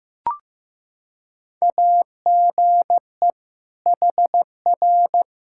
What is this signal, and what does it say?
Morse code, heard as a keyed tone, about 15 wpm, 700 Hz. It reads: AGE HR
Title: Practice QSO - Exchange 3 - More Info - 15wpm